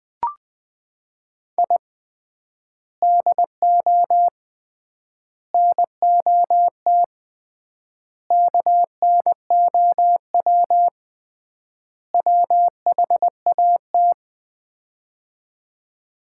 Morse code, heard as a keyed tone, about 20 wpm, 700 Hz. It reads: I DO NOT KNOW WHAT
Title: 3x Word-Spacing - Sets of 5 Words - 20wpm